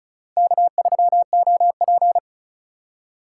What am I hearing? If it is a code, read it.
Morse code, heard as a keyed tone, about 45 wpm, 700 Hz. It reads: K3OP